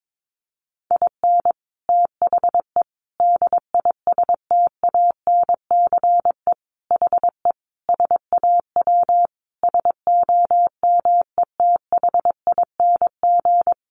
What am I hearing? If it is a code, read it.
Morse code, heard as a keyed tone, about 22 wpm, 700 Hz. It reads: IN THE DISTANCE HE SAW SOMETHING
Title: Sentences from Top 1200 Words 22wpm